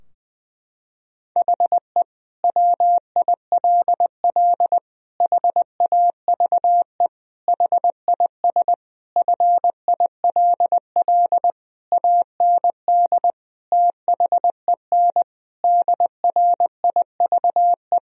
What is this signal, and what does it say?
Morse code, heard as a keyed tone, about 20 wpm, 700 Hz. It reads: HE WILL HAVE HIS FILL AND THEN DRIVE
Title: Sentences from Top 1000 Words - Review Part 3 - 20wpm